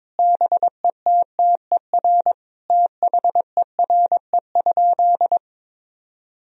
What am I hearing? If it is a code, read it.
Morse code, heard as a keyed tone, about 22 wpm, 700 Hz. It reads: BETTER THERE?
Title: Sentences from Top 1000 Words - Review Part 4 - 22wpm